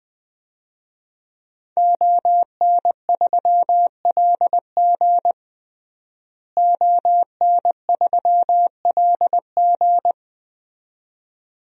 Morse code, heard as a keyed tone, about 20 wpm, 700 Hz. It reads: ON3LG ON3LG
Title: DXCC-100 Call Signs - 5 Characters - Repeated 2x - 20wpm